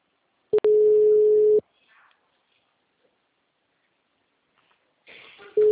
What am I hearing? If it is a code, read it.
Ringback tone (Europe, 425 Hz).